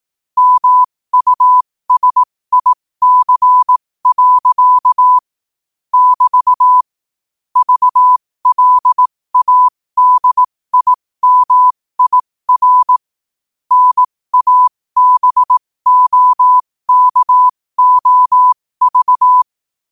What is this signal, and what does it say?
Morse code, heard as a keyed tone, about 18 wpm, spaced as for 15 wpm, 1000 Hz. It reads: MUSIC. <BT> VLADIMIR NABOKOV
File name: qotdfiles / 15wpm